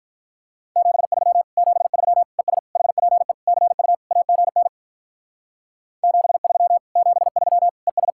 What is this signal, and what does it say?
Morse code, heard as a keyed tone, about 44 wpm, 700 Hz. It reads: 73 73 ES HPE CU AGN 73 73 ES